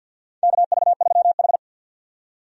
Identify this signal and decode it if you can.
Morse code, heard as a keyed tone, about 50 wpm, 700 Hz. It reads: KV3H
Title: USA Call Signs - 4 Characters - 50wpm